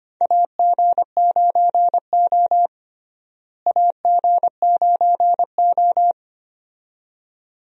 Morse code, heard as a keyed tone, about 25 wpm, 700 Hz. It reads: AG9O AG9O
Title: USA Call Signs - 4 Characters - Repeated 2x - 25wpm